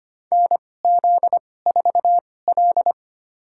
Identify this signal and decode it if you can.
Morse code, heard as a keyed tone, about 25 wpm, spaced as for 18 wpm, 700 Hz. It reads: NZ4L
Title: USA Call Signs - 2x Character Spacing - 25wpm